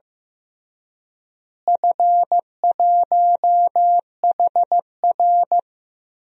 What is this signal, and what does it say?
Morse code, heard as a keyed tone, about 16 wpm, 700 Hz. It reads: F1HR
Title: DXCC-100 Call Signs - 4 Characters - Repeated 3x - 15wpm